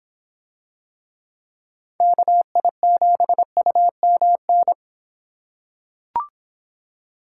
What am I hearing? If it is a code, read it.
Morse code, heard as a keyed tone, about 26 wpm, 700 Hz. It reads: KI7UMN